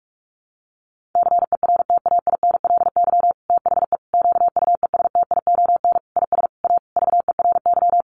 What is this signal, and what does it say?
Morse code, heard as a keyed tone, about 44 wpm, 700 Hz. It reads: CERTAINLY THE QUESTION IS A VERY
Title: Sentences from Top 1200 Words 45wpm